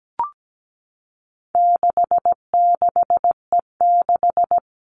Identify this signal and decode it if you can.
Morse code, heard as a keyed tone, about 17 wpm, 700 Hz. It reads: 66E6
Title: All Cut Numbers - 4-Digit 17wpm